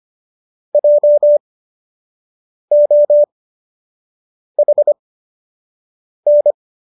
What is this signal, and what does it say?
Morse code, heard as a keyed tone, about 25 wpm, spaced as for 6 wpm, 590 Hz. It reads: JOHN